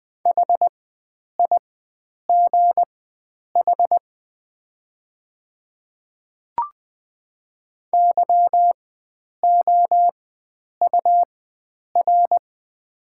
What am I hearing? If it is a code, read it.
Morse code, heard as a keyed tone, about 21 wpm, spaced as for 9 wpm, 690 Hz. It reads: HIGH YOUR
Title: 4x Character Spacing - Top 300 Words - 20wpm